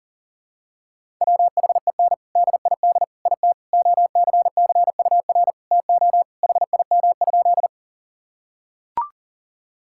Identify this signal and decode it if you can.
Morse code, heard as a keyed tone, about 40 wpm, 700 Hz. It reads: WHEN DID IT OCCUR TO HIM?
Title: Sentences from Top 1000 Words - Review Part 6 - 40wpm